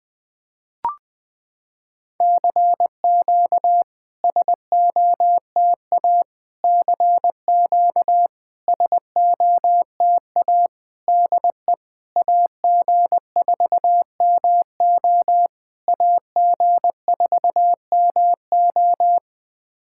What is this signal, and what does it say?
Morse code, heard as a keyed tone, about 20 wpm, 700 Hz. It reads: CQ SOTA CQ SOTA DE AG4MO AG4MO
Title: SOTA QSO Practice 20wpm